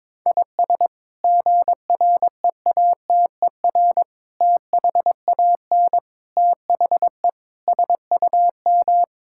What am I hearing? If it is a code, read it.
Morse code, heard as a keyed tone, about 22 wpm, 700 Hz. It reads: IS GREATER THAN THE SUM
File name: practice / 22wpm / Sets of 8 Words - 22wpm